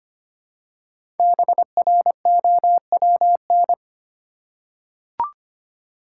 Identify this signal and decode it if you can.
Morse code, heard as a keyed tone, about 25 wpm, 700 Hz. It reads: BROWN
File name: 5 Letter Words - First 3 Letters Given - 25wpm